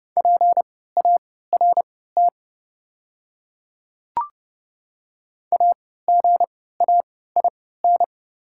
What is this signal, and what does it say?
Morse code, heard as a keyed tone, about 31 wpm, spaced as for 17 wpm, 700 Hz. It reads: PART AGAIN
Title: 3x Character Spacing - Top 300 Words - 30wpm